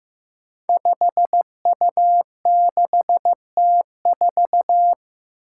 Morse code, heard as a keyed tone, about 15 wpm, 700 Hz. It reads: 5U6T4
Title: Five Challenging Letter-Number Combinations 15wpm